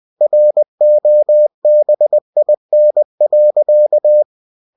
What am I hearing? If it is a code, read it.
Morse code, heard as a keyed tone, about 20 wpm, 600 Hz. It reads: ROBIN.
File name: cw-books / winnie-the-pooh / winnie-20wpm-page-0073